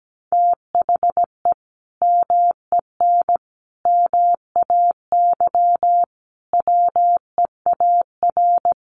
Morse code, heard as a keyed tone, about 17 wpm, 700 Hz. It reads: THE MEN MAY WEAR
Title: Sentences from Top 1500 Words 17wpm